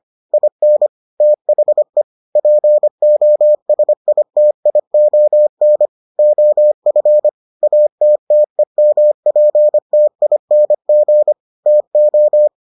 Morse code, heard as a keyed tone, about 25 wpm, 600 Hz. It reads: IN THE POSITION OF ATTEMPTING TO